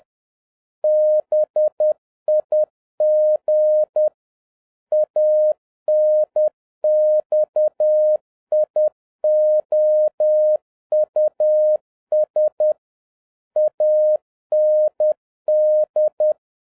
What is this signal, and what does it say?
Morse code, heard as a keyed tone, about 10 wpm, 610 Hz. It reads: BIG ANXIOUS AND
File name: 2011-07-mp3 / n56 BACK FROM A BIG ACCIDENT 10wpm